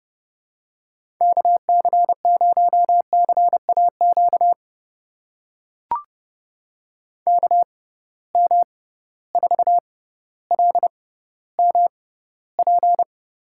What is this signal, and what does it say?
Morse code, heard as a keyed tone, about 30 wpm, spaced as for 15 wpm, 700 Hz. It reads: KC0CAQ KM4LMP
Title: USA Call Signs - 6 Characters - 6x Character Spacing - 30wpm